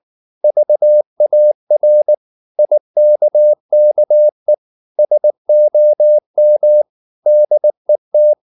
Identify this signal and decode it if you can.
Morse code, heard as a keyed tone, about 19 wpm, 600 Hz. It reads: VAR IKKE SOM DET